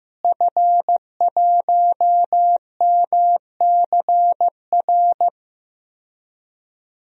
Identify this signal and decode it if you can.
Morse code, heard as a keyed tone, about 15 wpm, 700 Hz. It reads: F1MCR
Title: DXCC-100 Call Signs - 5 Characters - Repeated 2x - 15wpm